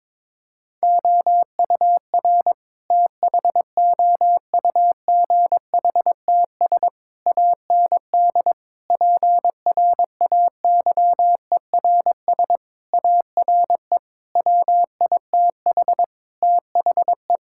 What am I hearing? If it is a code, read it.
Morse code, heard as a keyed tone, about 22 wpm, 700 Hz. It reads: OUR THOUGHTS AND PRAYERS ARE WITH THE
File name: practice / 22wpm / Sets of 7 Words - 22wpm